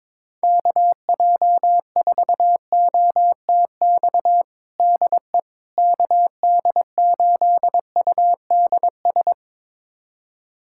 Morse code, heard as a keyed tone, about 21 wpm, 700 Hz. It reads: KJ4OTX DE KD8UDS
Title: USA Callsign DE Callsign - 6 Characters - 22wpm